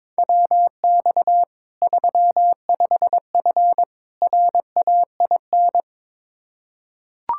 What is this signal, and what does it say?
Morse code, heard as a keyed tone, about 22 wpm, 700 Hz. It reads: WX 35F RAIN
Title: K1USN SST QSO Practice 22wpm